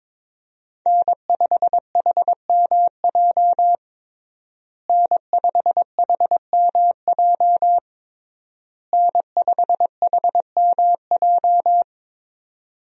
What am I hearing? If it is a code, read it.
Morse code, heard as a keyed tone, about 22 wpm, 700 Hz. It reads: N5HMJ N5HMJ N5HMJ
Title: USA Call Signs - 5 Characters - Repeated 3x - 22wpm